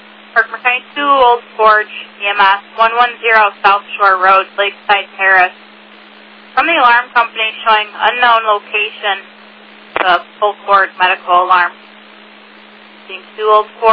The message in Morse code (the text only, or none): none